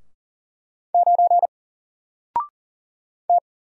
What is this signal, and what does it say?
Morse code, heard as a keyed tone, about 39 wpm, 700 Hz. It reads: 9 T
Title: Single Letter-Number Alt 40wpm